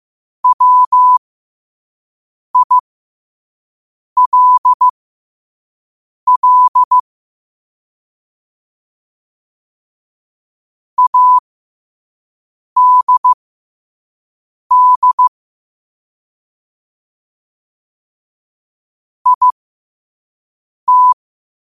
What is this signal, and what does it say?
Morse code, heard as a keyed tone, about 15 wpm, spaced as for 5 wpm, 1000 Hz. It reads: WILL ADD IT
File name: qotdfiles / 5wpm